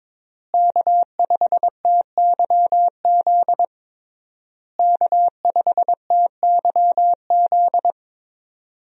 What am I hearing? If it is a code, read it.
Morse code, heard as a keyed tone, about 22 wpm, 700 Hz. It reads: K5TYZ K5TYZ
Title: USA Call Signs - 5 Characters - Repeated 3x - 22wpm